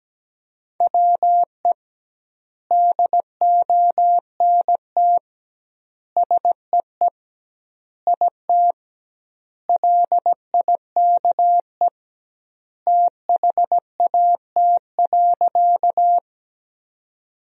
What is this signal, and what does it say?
Morse code, heard as a keyed tone, about 17 wpm, 700 Hz. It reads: WE DONT SEE IT LIKE THAT.
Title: 2x Word-Spacing - Sentences from Top 300 Words - 17wpm